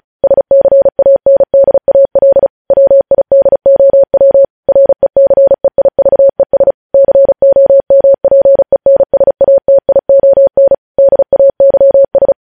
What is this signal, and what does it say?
Morse code, heard as a keyed tone, about 36 wpm, 550 Hz. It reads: SCANDAL WIDOW RECEIVES COMPENSATION DAYS
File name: podcast / News headlines-1x-35x35